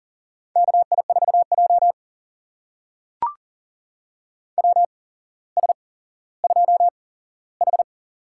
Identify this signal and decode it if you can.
Morse code, heard as a keyed tone, about 40 wpm, spaced as for 17 wpm, 700 Hz. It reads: KI4J WS2H
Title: USA Call Signs - 4 Characters - 8x Character Spacing - 40wpm